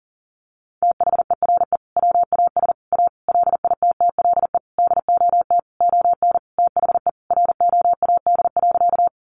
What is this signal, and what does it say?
Morse code, heard as a keyed tone, about 40 wpm, 700 Hz. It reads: THERE WAS A LITTLE DOT ON THE ROAD.
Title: Sentences from Top 1500 Words 40wpm